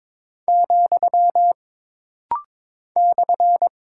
Morse code, heard as a keyed tone, about 22 wpm, 700 Hz. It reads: , /
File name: practice / 22wpm / Single Punctuation-Prosign - Rapid-Fire - 22wpm